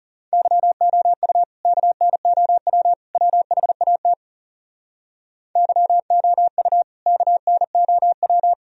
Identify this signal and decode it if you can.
Morse code, heard as a keyed tone, about 40 wpm, 700 Hz. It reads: YOU KNOW WHAT YOU KNOW